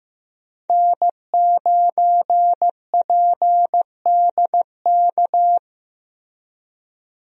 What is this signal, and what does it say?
Morse code, heard as a keyed tone, about 15 wpm, 700 Hz. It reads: N9PDK
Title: USA Call Signs - 5 Characters - 15wpm